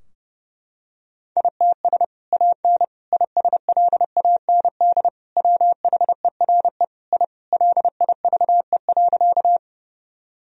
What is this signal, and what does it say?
Morse code, heard as a keyed tone, about 30 wpm, 700 Hz. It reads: ITS AN ISLAND WHERE I LIVE.
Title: Sentences from Top 1000 Words - Review Part 3 - 30wpm